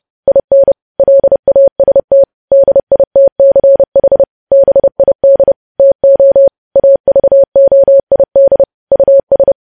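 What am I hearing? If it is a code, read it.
Morse code, heard as a keyed tone, about 30 wpm, 550 Hz. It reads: IN LAST DITCH BID TO AVOID US
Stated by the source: News headlines sent as Morse practice at 30.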